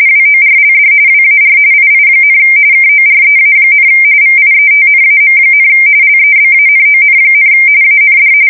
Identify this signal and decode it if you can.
RTTY, about 45.45 baud. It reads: WORLD WAR II, KOREA, VIETNAM, LEBANON, AND
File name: fsk170 / USS-New-Jersey